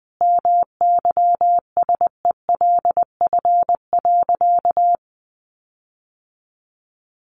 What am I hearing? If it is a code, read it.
Morse code, heard as a keyed tone, about 20 wpm, 700 Hz. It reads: MYSELF.
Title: Sentences from Top 1300 Words 20wpm